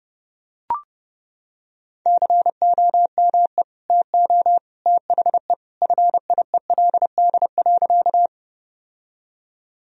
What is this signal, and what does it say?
Morse code, heard as a keyed tone, about 30 wpm, 700 Hz. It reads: COME TO THE FIELD.
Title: Sentences from Top 1000 Words - Review Part 2 - 30wpm